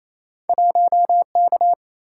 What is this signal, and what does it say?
Morse code, heard as a keyed tone, about 28 wpm, 700 Hz. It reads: 1K
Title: Single-Letter Single-Number 28wpm